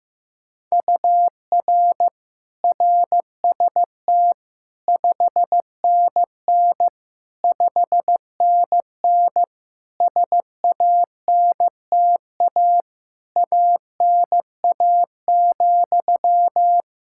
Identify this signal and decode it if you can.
Morse code, heard as a keyed tone, about 15 wpm, 700 Hz. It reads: UR RST 5NN 5NN SANTA ANA,
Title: Practice QSO - Exchange 1 - Beginning - 15wpm